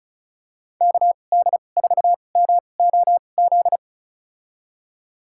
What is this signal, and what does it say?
Morse code, heard as a keyed tone, about 34 wpm, spaced as for 25 wpm, 700 Hz. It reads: KD4MOZ
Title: USA Call Signs - 6 Characters - 2x Character Spacing - 35wpm